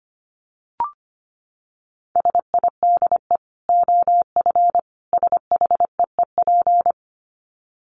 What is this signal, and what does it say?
Morse code, heard as a keyed tone, about 25 wpm, 700 Hz. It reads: SIDE OF SHEEP
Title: Dit Challenge 25wpm